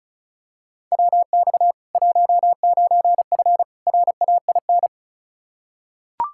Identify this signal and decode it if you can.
Morse code, heard as a keyed tone, about 36 wpm, 700 Hz. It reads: WX 19F RAIN
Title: K1USN SST QSO Practice 35wpm